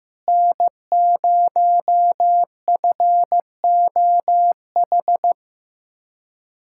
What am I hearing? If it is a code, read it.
Morse code, heard as a keyed tone, about 15 wpm, 700 Hz. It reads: N0FOH
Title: USA Callsign DE Callsign - 5 Characters - 15wpm